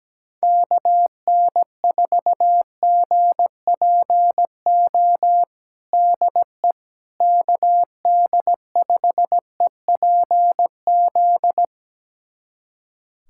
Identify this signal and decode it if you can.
Morse code, heard as a keyed tone, about 17 wpm, 700 Hz. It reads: KN4GPO DE KD5EPZ
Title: USA Callsign DE Callsign - 6 Characters - 17wpm